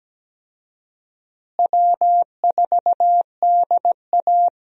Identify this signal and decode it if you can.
Morse code, heard as a keyed tone, about 17 wpm, 700 Hz. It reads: W4DA
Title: USA Call Signs - 4 Characters - Repeated 2x - 17wpm